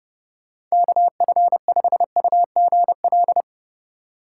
Morse code, heard as a keyed tone, about 32 wpm, 700 Hz. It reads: KF5UGL
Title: USA Call Signs - 6 Characters - 4x Character Spacing - 30wpm